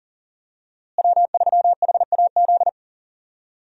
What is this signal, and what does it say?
Morse code, heard as a keyed tone, about 40 wpm, 700 Hz. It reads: W3HAZ
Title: USA Call Signs - 5 Characters - Repeated 3x - 40wpm